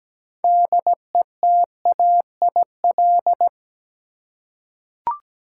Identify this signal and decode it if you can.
Morse code, heard as a keyed tone, about 17 wpm, 700 Hz. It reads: DETAIL
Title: Six Letter Words - 17wpm